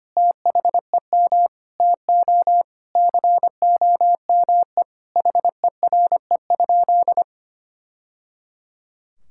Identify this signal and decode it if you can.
Morse code, heard as a keyed tone, about 25 wpm, 700 Hz. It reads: THEM TO COME HERE?
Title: Sentences from Top 1000 Words - Review Part 5 - 25wpm